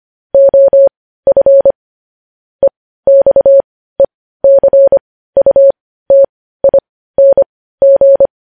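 Morse code, heard as a keyed tone, about 25 wpm, spaced as for 15 wpm, 550 Hz. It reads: OF EXECUTING
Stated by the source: Morse code news headlines at 15 WPM.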